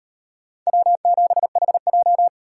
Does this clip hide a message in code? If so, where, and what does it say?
Yes: Morse code from 0.7 to 2.3 s, heard as a keyed tone, about 38 wpm, 700 Hz. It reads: W7HJ